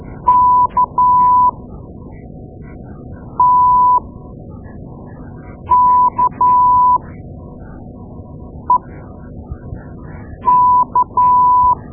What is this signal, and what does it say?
Morse code, heard as a keyed tone, about 12 wpm, spaced as for 4 wpm, 990 Hz. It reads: KTKEK
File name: Morse Lessons / Morse Lessons / Tape-2A